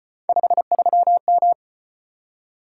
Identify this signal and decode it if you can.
Morse code, heard as a keyed tone, about 33 wpm, 700 Hz. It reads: 53M